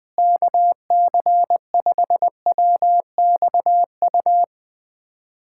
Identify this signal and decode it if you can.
Morse code, heard as a keyed tone, about 20 wpm, 700 Hz. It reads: KC5WXU
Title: USA Callsign DE Callsign - 6 Characters - 20wpm